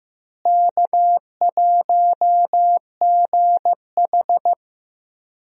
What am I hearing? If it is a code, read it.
Morse code, heard as a keyed tone, about 15 wpm, 700 Hz. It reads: K1GH
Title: USA Call Signs - 4 Characters - Repeated 3x - 15wpm